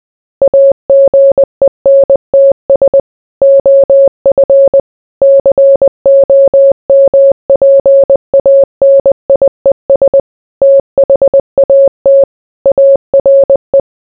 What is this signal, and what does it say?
Morse code, heard as a keyed tone, about 20 wpm, 550 Hz. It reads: AGENTS OF COMPANIES THAT ARE